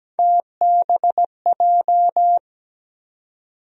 Morse code, heard as a keyed tone, about 17 wpm, 700 Hz. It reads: TBJ
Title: USA Call Sign Postfixes - 3 Letters 17wpm